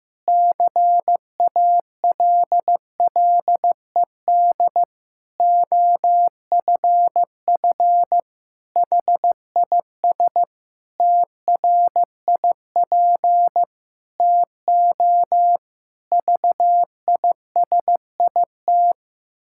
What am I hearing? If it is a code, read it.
Morse code, heard as a keyed tone, about 15 wpm, 700 Hz. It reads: CALLED OFF HIS TRIP TO VISIT
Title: Pangram Challenge v2 - 15wpm